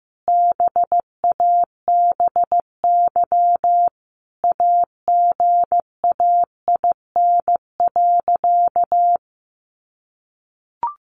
Morse code, heard as a keyed tone, about 15 wpm, 700 Hz. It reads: BABY AGAIN.
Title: Sentences from Top 1300 Words 15wpm